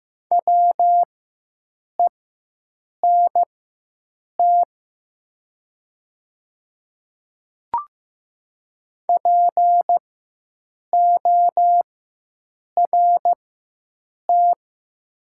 Morse code, heard as a keyed tone, about 15 wpm, spaced as for 7 wpm, 700 Hz. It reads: WENT PORT
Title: 4x Character Spacing - Top 200 Words - 15wpm